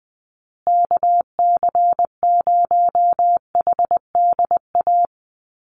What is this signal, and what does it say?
Morse code, heard as a keyed tone, about 20 wpm, 700 Hz. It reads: KC0HDA
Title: USA Call Signs Repeated 2x 20wpm